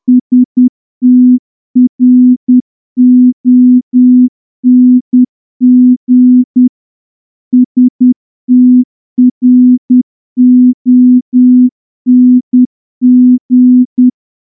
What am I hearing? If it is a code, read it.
Morse code, heard as a keyed tone, about 10 wpm, 250 Hz. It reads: STRONG STRONG
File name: morse-training-10WPM